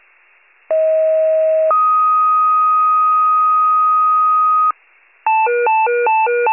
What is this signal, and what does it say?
Two-tone page, heard as a one-second tone, then a three-second tone, about 640 then 1250 Hz. It is heard from 0.7 to 4.7 s.
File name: wcalert